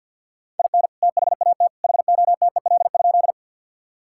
Morse code, heard as a keyed tone, about 50 wpm, 700 Hz. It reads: IN THAT HOTEL?
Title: Sentences from Top 1600 Words 50wpm